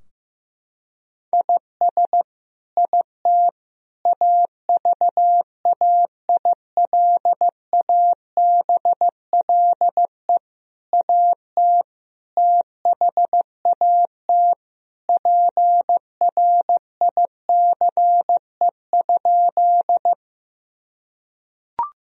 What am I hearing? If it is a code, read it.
Morse code, heard as a keyed tone, about 15 wpm, 700 Hz. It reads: IS IT AVAILABLE AT THAT PRICE?
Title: Sentences from Top 1000 Words - Review Part 3 - 15wpm